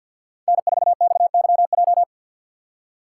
Morse code, heard as a keyed tone, about 52 wpm, 690 Hz. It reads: N4XYJ
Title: USA Call Signs - 6x Character Spacing - 50wpm